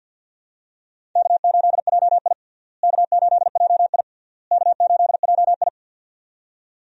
Morse code, heard as a keyed tone, about 50 wpm, 700 Hz. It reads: K8JI K8JI K8JI
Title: USA Call Signs - 4 Characters - Repeated 3x - 50wpm